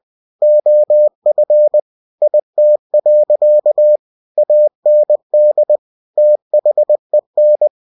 Morse code, heard as a keyed tone, about 20 wpm, 600 Hz. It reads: OF IT. AND THEN
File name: cw-books / winnie-the-pooh / winnie-20wpm-page-0000